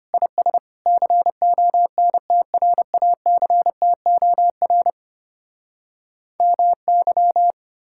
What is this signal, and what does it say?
Morse code, heard as a keyed tone, about 33 wpm, 700 Hz. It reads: IS CONTRACTOR MY